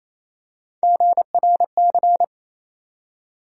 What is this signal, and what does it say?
Morse code, heard as a keyed tone, about 28 wpm, 700 Hz. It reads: GRC
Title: USA Call Sign Postfixes - 3 Letters 28wpm